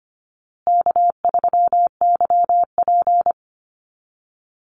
Morse code, heard as a keyed tone, about 25 wpm, 700 Hz. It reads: K3YP
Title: Call Signs Repeated 2x - 25wpm